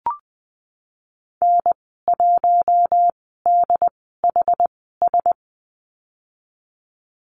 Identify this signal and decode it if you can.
Morse code, heard as a keyed tone, about 19 wpm, spaced as for 14 wpm, 700 Hz. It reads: N1DHS
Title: 167 Lesson 27g - Number 7 - Call Signs 20wpm